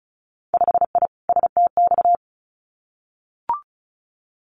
Morse code, heard as a keyed tone, about 36 wpm, 700 Hz. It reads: 5I STX
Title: Winter Field Day Exchanges Encore 35wpm